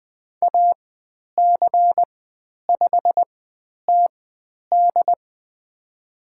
Morse code, heard as a keyed tone, about 20 wpm, spaced as for 10 wpm, 700 Hz. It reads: AC5TD
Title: 095 Lesson 16g - Letter c - Call Signs 20wpm